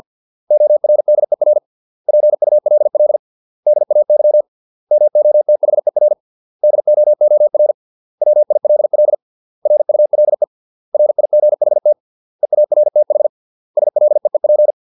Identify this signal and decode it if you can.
Morse code, heard as a keyed tone, about 50 wpm, 600 Hz. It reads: ORDER PULL DAY MOTHER DOOR WILL RULE RIGHT EARTH SLEEP